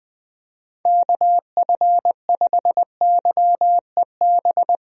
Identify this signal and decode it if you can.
Morse code, heard as a keyed tone, about 20 wpm, 700 Hz. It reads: KF5YEB
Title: USA Call Signs - 6 Characters - 20wpm